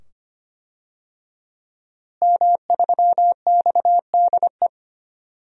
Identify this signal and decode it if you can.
Morse code, heard as a keyed tone, about 25 wpm, 700 Hz. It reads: M3XDE
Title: DXCC-100 Call Signs - 5 Characters - Repeated 2x - 25wpm